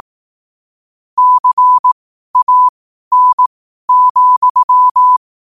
Morse code, heard as a keyed tone, about 18 wpm, spaced as for 12 wpm, 1000 Hz. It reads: CAN,
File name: qotdfiles / 12wpm